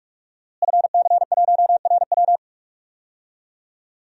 Morse code, heard as a keyed tone, about 45 wpm, 700 Hz. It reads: FC1RW
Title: DXCC-100 Call Signs - 5 Characters - Repeated 3x - 45wpm